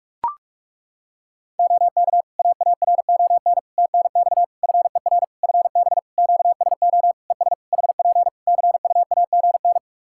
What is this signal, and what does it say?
Morse code, heard as a keyed tone, about 41 wpm, 700 Hz. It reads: OK AARON TNX FER FB QSO ES HP CUAGN
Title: K1USN SST QSO Practice 45wpm